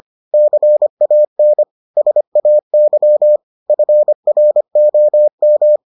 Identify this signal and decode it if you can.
Morse code, heard as a keyed tone, about 25 wpm, 600 Hz. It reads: CAN SAY FROM